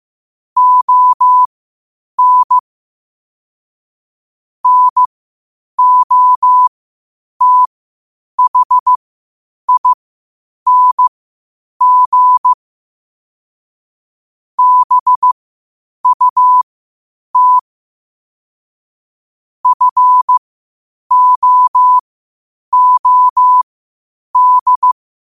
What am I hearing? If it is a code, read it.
Morse code, heard as a keyed tone, about 15 wpm, spaced as for 8 wpm, 1000 Hz. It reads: ON NOTHING BUT FOOD
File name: qotdfiles / 8wpm